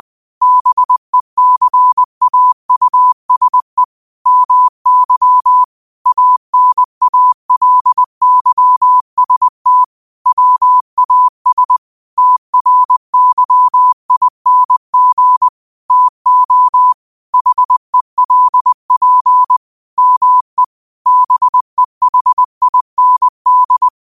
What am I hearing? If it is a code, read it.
Morse code, heard as a keyed tone, about 20 wpm, 1000 Hz. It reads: BECAUSE MY ANALYST WAS TRYING TO HELP ME BEHIND
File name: qotdfiles / 20wpm